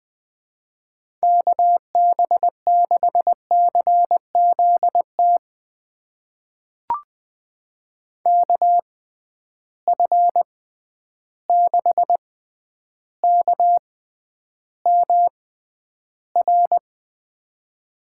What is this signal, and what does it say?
Morse code, heard as a keyed tone, about 20 wpm, spaced as for 10 wpm, 700 Hz. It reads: KB6CZT KF6KMR
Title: USA Call Signs - 6 Characters - 6x Character Spacing - 20wpm